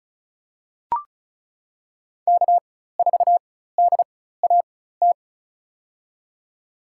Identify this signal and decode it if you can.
Morse code, heard as a keyed tone, about 34 wpm, spaced as for 16 wpm, 700 Hz. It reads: K4DAT
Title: USA Call Signs - 5 Characters - 4x Character Spacing - 35wpm